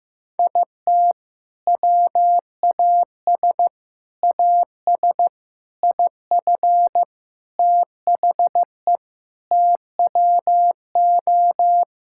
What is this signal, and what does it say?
Morse code, heard as a keyed tone, about 15 wpm, 700 Hz. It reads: IT WAS AS IF THE TWO
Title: Sentences from Top 1000 Words - Review Part 6 - 15wpm